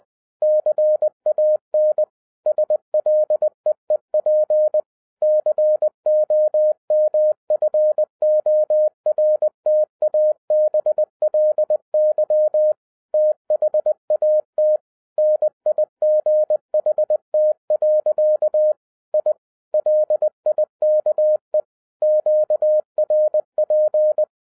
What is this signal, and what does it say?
Morse code, heard as a keyed tone, about 20 wpm, 610 Hz. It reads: CAN SLEEP COMFORTABLY THAT NIGHT. I LIKE QRP
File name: 2011-10-mp3 / n13-TIPS FROM MY 34 YEARS OF EXPERIENCE 20wpm